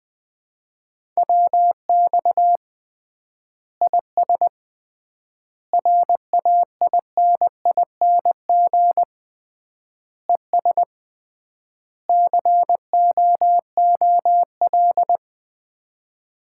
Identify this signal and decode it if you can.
Morse code, heard as a keyed tone, about 20 wpm, 700 Hz. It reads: WX IS RAINING ES COOL
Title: Common QSO Phrases - 3x Word-Spacing - 20wpm